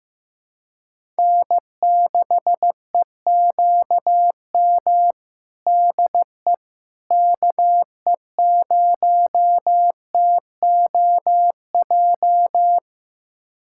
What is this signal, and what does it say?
Morse code, heard as a keyed tone, about 15 wpm, 700 Hz. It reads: N6EQM DE KE0TOJ
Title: USA Callsign DE Callsign 15wpm